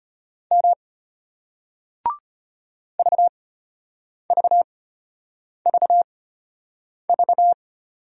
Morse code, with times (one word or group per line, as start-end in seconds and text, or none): 0.5-0.7 M
3.0-7.5 VVVV